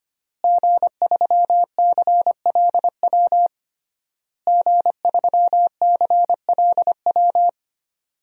Morse code, heard as a keyed tone, about 25 wpm, 700 Hz. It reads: G3CLW G3CLW
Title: DXCC-100 Call Signs - 5 Characters - Repeated 3x - 25wpm